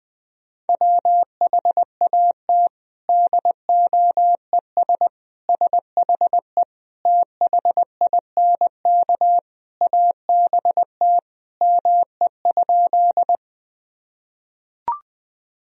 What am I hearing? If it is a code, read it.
Morse code, heard as a keyed tone, about 21 wpm, 700 Hz. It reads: WHAT DOES SHE THINK ABT ME?
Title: Sentences from Top 200 Words and Abbreviations 20wpm